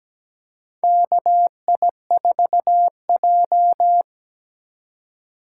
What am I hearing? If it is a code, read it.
Morse code, heard as a keyed tone, about 17 wpm, 700 Hz. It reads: KI4J
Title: USA Call Signs - 4 Characters - 3x Character Spacing - 17wpm